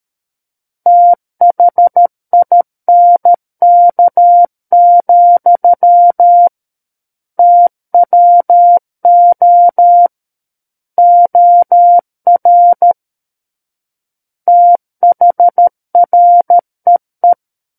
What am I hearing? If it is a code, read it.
Morse code, heard as a keyed tone, about 13 wpm, 700 Hz. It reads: THINK, TWO OR THREE
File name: Book One / 04-Chapter 4 - 13 WPM